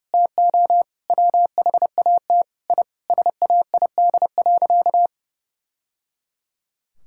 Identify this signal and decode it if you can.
Morse code, heard as a keyed tone, about 30 wpm, 700 Hz. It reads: TO WHAT I SAID.
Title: Sentences from Top 1000 Words - Review Part 6 - 30wpm